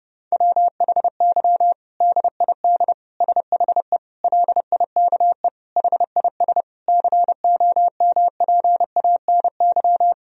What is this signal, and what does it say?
Morse code, heard as a keyed tone, about 30 wpm, 700 Hz. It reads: WHY DID SHE LIKE HIS COMPANY
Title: Sentences from Top 1000 Words - Review Part 3 - 30wpm